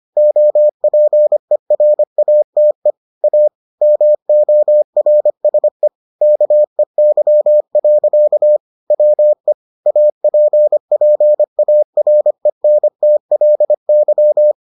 Morse code, heard as a keyed tone, about 25 wpm, 600 Hz. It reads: OPERATE A MORSE KEY. WE APPARENTLY